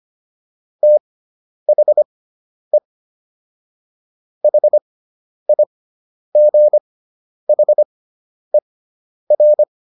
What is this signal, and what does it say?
Morse code, heard as a keyed tone, about 25 wpm, spaced as for 10 wpm, 600 Hz. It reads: THE HIGHER